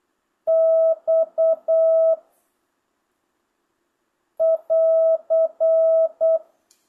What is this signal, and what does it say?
Morse code, heard as a keyed tone, about 8 wpm, 650 Hz. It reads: X <AR>
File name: jh8cbh / 2025cw